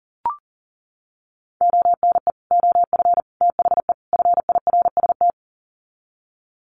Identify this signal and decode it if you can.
Morse code, heard as a keyed tone, about 40 wpm, 700 Hz. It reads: ONE OF THE FIRST
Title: Sets of 4 Words 40wpm